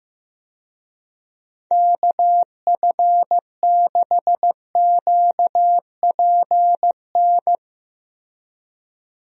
Morse code, heard as a keyed tone, about 15 wpm, 700 Hz. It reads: KF6QPN
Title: USA Call Signs - 6 Characters - Repeated 2x - 15wpm